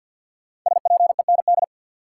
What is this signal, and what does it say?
Morse code, heard as a keyed tone, about 53 wpm, 700 Hz. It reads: SPEND
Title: Sentences from Top 800 Words - New Word Drill - 50wpm